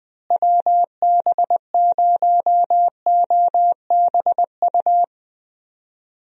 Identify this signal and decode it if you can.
Morse code, heard as a keyed tone, about 20 wpm, 700 Hz. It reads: WB0OBU
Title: USA Call Signs - 6 Characters - Repeated 3x - 20wpm